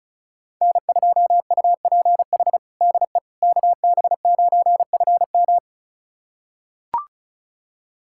Morse code, heard as a keyed tone, about 36 wpm, 700 Hz. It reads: N2UPH DE KB9FM
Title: USA Callsign DE Callsign - 5 Characters - 35wpm